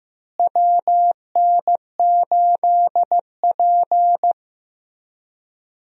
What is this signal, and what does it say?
Morse code, heard as a keyed tone, about 15 wpm, 700 Hz. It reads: WN8P
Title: USA Call Signs - 4 Characters - Repeated 3x - 15wpm